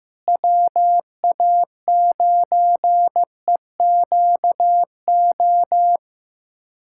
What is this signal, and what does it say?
Morse code, heard as a keyed tone, about 15 wpm, 700 Hz. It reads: WA9EQO
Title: USA Callsign DE Callsign 15wpm